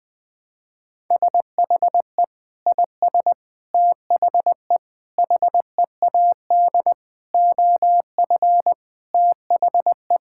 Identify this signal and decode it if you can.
Morse code, heard as a keyed tone, about 20 wpm, 700 Hz. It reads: SHE IS THE HEAD OF THE
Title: Sentences with 4-Word Phrases - 20wpm